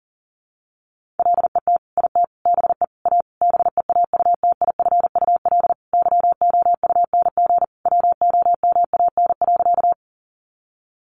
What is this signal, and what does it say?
Morse code, heard as a keyed tone, about 38 wpm, 700 Hz. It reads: LET IT BE A BEAUTIFUL YOUNG WOMAN.
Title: Sentences from Top 1100 Words 40wpm